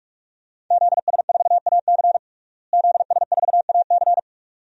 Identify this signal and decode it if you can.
Morse code, heard as a keyed tone, about 41 wpm, 700 Hz. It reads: ZS4AC ZS4AC
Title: DXCC-100 Call Signs - 5 Characters - Repeated 2x - 45wpm